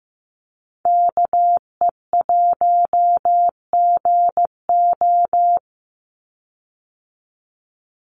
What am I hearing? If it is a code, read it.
Morse code, heard as a keyed tone, about 15 wpm, 700 Hz. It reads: KE1GO
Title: USA Call Signs 15wpm